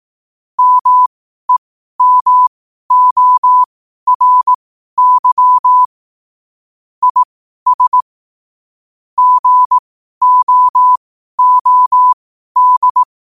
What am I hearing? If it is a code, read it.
Morse code, heard as a keyed tone, about 18 wpm, spaced as for 12 wpm, 1000 Hz. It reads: MEMORY IS GOOD
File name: qotdfiles / 12wpm